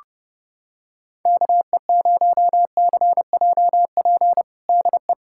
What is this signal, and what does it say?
Morse code, heard as a keyed tone, about 30 wpm, 700 Hz. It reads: KE0CJP DE
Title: USA Callsign DE Callsign - 6 Characters - 30wpm